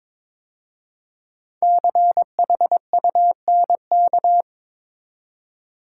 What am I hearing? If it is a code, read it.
Morse code, heard as a keyed tone, about 22 wpm, 690 Hz. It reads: CHUNK
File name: 5 Letter Words - First 3 Letters Given - 22wpm